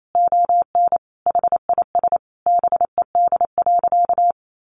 Morse code, heard as a keyed tone, about 28 wpm, 700 Hz. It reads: ON HIS BED.
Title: Sentences from Top 1200 Words 28wpm